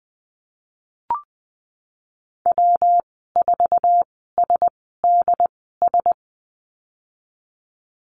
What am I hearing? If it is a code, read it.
Morse code, heard as a keyed tone, about 20 wpm, spaced as for 15 wpm, 700 Hz. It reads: W4SDS
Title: 140 Lesson 23g - Letter f - Call Signs 20wpm